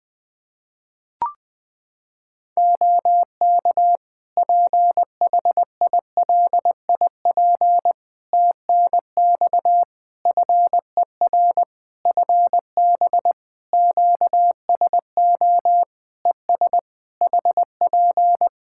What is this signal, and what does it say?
Morse code, heard as a keyed tone, about 20 wpm, 700 Hz. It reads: OK PHILIP TNX FER FB QSO ES HP
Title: K1USN SST QSO Practice 20wpm